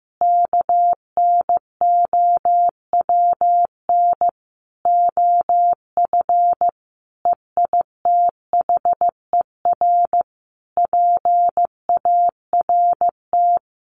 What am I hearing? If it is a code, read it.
Morse code, heard as a keyed tone, about 15 wpm, 700 Hz. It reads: KNOWN OF EITHER PART
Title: Sentences from Top 1500 Words 15wpm